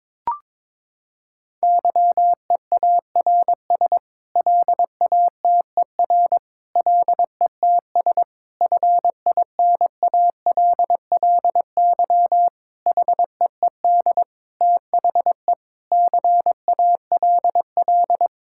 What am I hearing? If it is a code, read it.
Morse code, heard as a keyed tone, about 22 wpm, 700 Hz. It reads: YEARS LATER LETS FINALLY HEED THE CALL
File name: practice / 22wpm / Sets of 8 Words - 22wpm